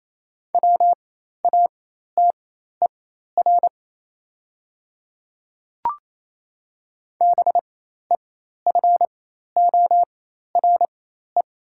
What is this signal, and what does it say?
Morse code, heard as a keyed tone, about 28 wpm, spaced as for 13 wpm, 700 Hz. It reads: WATER BEFORE
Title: 4x Character Spacing - Top 200 Words - 28wpm